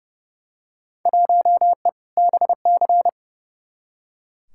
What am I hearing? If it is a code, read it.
Morse code, heard as a keyed tone, about 30 wpm, 700 Hz. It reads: 1E BC
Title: Field Day Exchanges - 30wpm